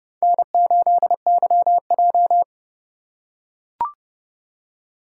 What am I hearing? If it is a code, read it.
Morse code, heard as a keyed tone, about 30 wpm, 700 Hz. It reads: N8YJ
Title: USA Call Signs - 4 Characters - 4x Character Spacing - 30wpm